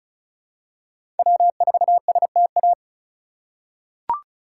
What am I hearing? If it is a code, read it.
Morse code, heard as a keyed tone, about 34 wpm, 700 Hz. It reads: W4STA
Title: USA Call Signs - 5 Characters - 4x Character Spacing - 35wpm